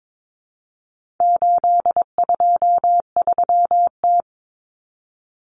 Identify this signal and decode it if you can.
Morse code, heard as a keyed tone, about 22 wpm, 700 Hz. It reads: 823T
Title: Common Cut Numbers - 4-Digit 22wpm